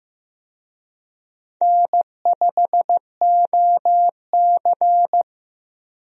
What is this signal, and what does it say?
Morse code, heard as a keyed tone, about 15 wpm, 700 Hz. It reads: N5OC
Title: USA Call Signs - 4 Characters - Repeated 2x - 15wpm